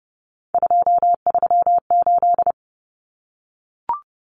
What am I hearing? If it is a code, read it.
Morse code, heard as a keyed tone, about 30 wpm, 700 Hz. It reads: 238
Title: Numbers - 3-Digit 30wpm